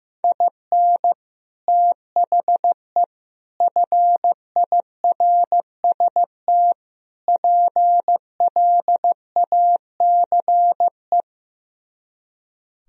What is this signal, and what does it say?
Morse code, heard as a keyed tone, about 15 wpm, 700 Hz. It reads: IN THE FIRST PLACE
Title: Speed-Racing - Sets of 4 Words 15wpm